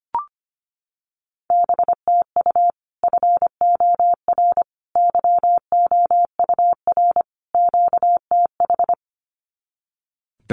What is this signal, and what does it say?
Morse code, heard as a keyed tone, about 25 wpm, 700 Hz. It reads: BTU FOR YOUR QTH
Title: Common QSO Phrases 25wpm